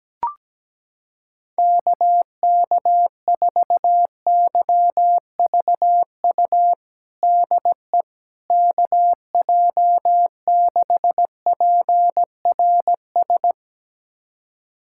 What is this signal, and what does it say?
Morse code, heard as a keyed tone, about 17 wpm, 700 Hz. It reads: KK4YVU DE KJ6PRS
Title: USA Callsign DE Callsign 17wpm